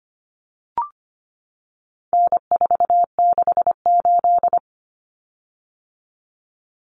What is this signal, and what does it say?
Morse code, heard as a keyed tone, about 25 wpm, 700 Hz. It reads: N468
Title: Common Cut Numbers - 4-Digit 25wpm